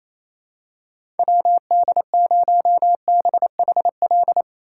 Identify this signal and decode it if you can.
Morse code, heard as a keyed tone, about 27 wpm, 700 Hz. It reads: WD0BHL
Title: USA Call Signs - 6 Characters - 8x Character Spacing - 28wpm